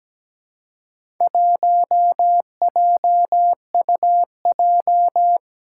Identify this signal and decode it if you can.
Morse code, heard as a keyed tone, about 17 wpm, 700 Hz. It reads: 1JUJ
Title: Four Challenging Letter-Number Combinations 17wpm